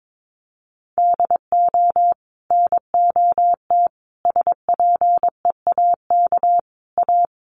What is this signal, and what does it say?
Morse code, heard as a keyed tone, about 22 wpm, 700 Hz. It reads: DO NOT SPEAK A
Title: Sentences from Top 1100 Words 22wpm